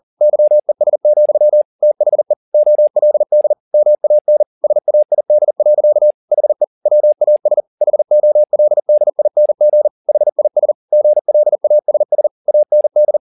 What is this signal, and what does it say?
Morse code, heard as a keyed tone, about 40 wpm, 600 Hz. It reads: YES, THE OLD MAN SAID. HE WAS HOLDING HIS GLASS AND
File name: the old man and the sea 1 40wpm